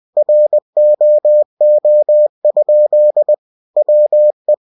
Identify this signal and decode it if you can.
Morse code, heard as a keyed tone, about 20 wpm, 600 Hz. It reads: ROO? WE
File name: cw-books / winnie-the-pooh / winnie-20wpm-page-0061